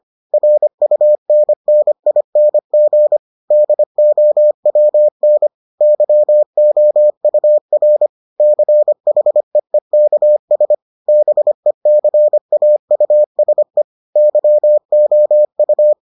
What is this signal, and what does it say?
Morse code, heard as a keyed tone, about 25 wpm, 600 Hz. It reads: RUNNING DOWN YOUR CHEEKS BECAUSE YOU